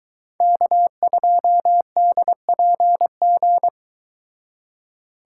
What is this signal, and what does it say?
Morse code, heard as a keyed tone, about 23 wpm, 700 Hz. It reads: K2DPG